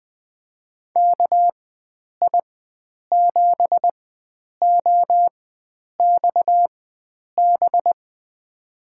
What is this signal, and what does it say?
Morse code, heard as a keyed tone, about 20 wpm, spaced as for 9 wpm, 700 Hz. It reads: KI7OXB
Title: USA Call Signs - 6 Characters - 4x Character Spacing - 20wpm